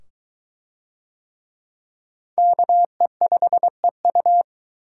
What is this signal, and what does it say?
Morse code, heard as a keyed tone, about 23 wpm, 700 Hz. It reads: KE5EU